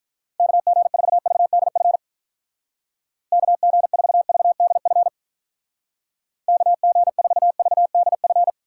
Morse code, heard as a keyed tone, about 44 wpm, 700 Hz. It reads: KG4VDF KG4VDF KG4VDF